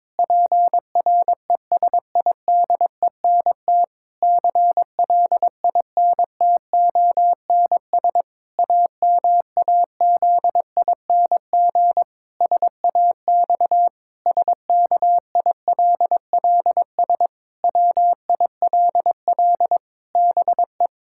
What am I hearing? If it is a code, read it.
Morse code, heard as a keyed tone, about 22 wpm, 700 Hz. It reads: PRESIDENT CLINTONS AMAZING SAX SKILLS WILL BE
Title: Pangram Challenge v2 - 22wpm